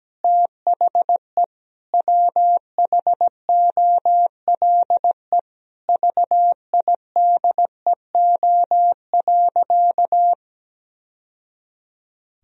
Morse code, heard as a keyed tone, about 17 wpm, 700 Hz. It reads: THE WHOLE VIDEO.
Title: Sentences from Top 1000 Words - Review Part 4 - 17wpm